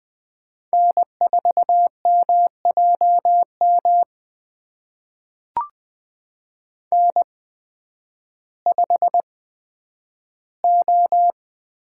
Morse code, heard as a keyed tone, about 20 wpm, spaced as for 11 wpm, 700 Hz. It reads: N4MJM N5O
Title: USA Call Signs - 8x Character Spacing - 20wpm